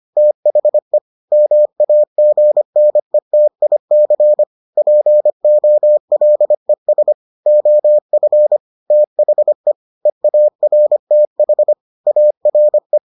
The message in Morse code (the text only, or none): THE MAGNETIC POLES OF THE EARTH ARE